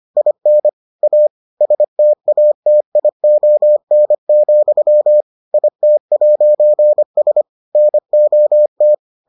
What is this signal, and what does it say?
Morse code, heard as a keyed tone, about 25 wpm, 600 Hz. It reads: IN A STATION, IT'S NOT